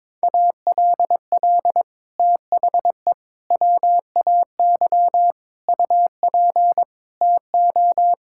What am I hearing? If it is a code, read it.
Morse code, heard as a keyed tone, about 22 wpm, 700 Hz. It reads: ALL THE WAY UP TO
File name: practice / 22wpm / Sets of 6 Words - 22wpm